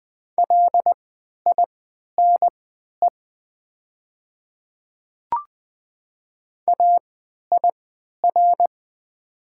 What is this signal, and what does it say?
Morse code, heard as a keyed tone, about 20 wpm, spaced as for 11 wpm, 700 Hz. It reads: LINE AIR
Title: 3x Character Spacing - Top 300 Words - 20wpm